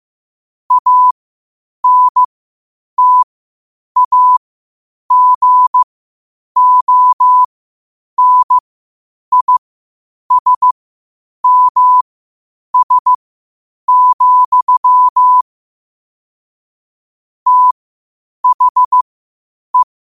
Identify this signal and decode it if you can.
Morse code, heard as a keyed tone, about 15 wpm, spaced as for 8 wpm, 1000 Hz. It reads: ANTAGONISMS, THE
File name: qotdfiles / 8wpm